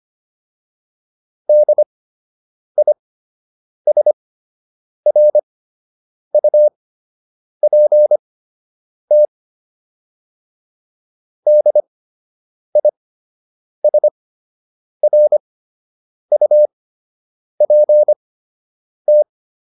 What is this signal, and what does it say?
Morse code, heard as a keyed tone, about 25 wpm, spaced as for 8 wpm, 600 Hz. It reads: DISRUPT DISRUPT